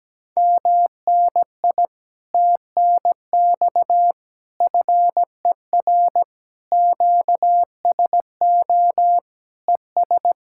Morse code, heard as a keyed tone, about 17 wpm, 700 Hz. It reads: MNI TNX FER QSO ES
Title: Speed-Racing Common QSO Phrases - 17wpm